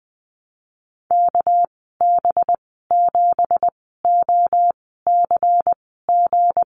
Morse code, heard as a keyed tone, about 20 wpm, spaced as for 15 wpm, 700 Hz. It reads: KB7OCG
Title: 238 Lesson 38g - Letter q - Call Signs 20wpm